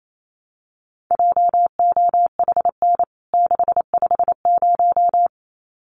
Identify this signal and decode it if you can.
Morse code, heard as a keyed tone, about 28 wpm, 700 Hz. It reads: JOHN 650
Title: CWT Exchange 28wpm